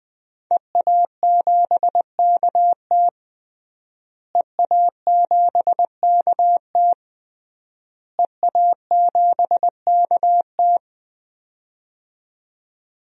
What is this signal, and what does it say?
Morse code, heard as a keyed tone, about 20 wpm, 700 Hz. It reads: EA7KT EA7KT EA7KT
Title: DXCC-100 Call Signs - 5 Characters - Repeated 3x - 20wpm